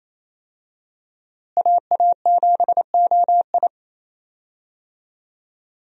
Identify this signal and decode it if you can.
Morse code, heard as a keyed tone, about 28 wpm, 700 Hz. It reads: AA7OI
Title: USA Call Signs - 5 Characters - Repeated 3x - 28wpm